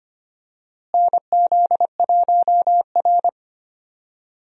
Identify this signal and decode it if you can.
Morse code, heard as a keyed tone, about 25 wpm, 700 Hz. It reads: NZ1R
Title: USA Call Signs - 4 Characters - Repeated 3x - 25wpm